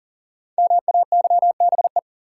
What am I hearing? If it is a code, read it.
Morse code, heard as a keyed tone, about 39 wpm, 700 Hz. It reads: MAYBE
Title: Five Letter Words - 40wpm